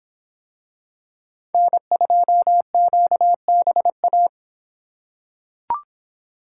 Morse code, heard as a keyed tone, about 24 wpm, 700 Hz. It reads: N2QBA